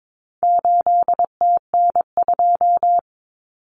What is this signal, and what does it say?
Morse code, heard as a keyed tone, about 22 wpm, 700 Hz. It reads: 8TN2
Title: Common Cut Numbers - 4-Digit 22wpm